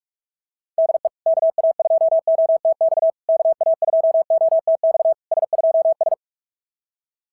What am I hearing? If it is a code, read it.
Morse code, heard as a keyed tone, about 44 wpm, 650 Hz. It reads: DE KA2OTX KA2OTX S2S
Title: SOTA QSO Practice 45wpm